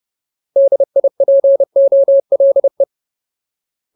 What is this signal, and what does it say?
Morse code, heard as a keyed tone, about 30 wpm, 550 Hz. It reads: DIPOLE
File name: Ultimate Word List / 30WPM Ultimate